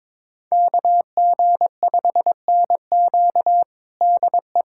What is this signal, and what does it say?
Morse code, heard as a keyed tone, about 22 wpm, 700 Hz. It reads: KG5NQ DE
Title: USA Callsign DE Callsign 22wpm